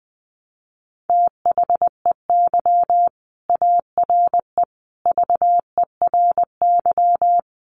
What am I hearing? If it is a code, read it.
Morse code, heard as a keyed tone, about 20 wpm, 700 Hz. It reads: THEY ARE VERY
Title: Sentences from Top 1200 Words 20wpm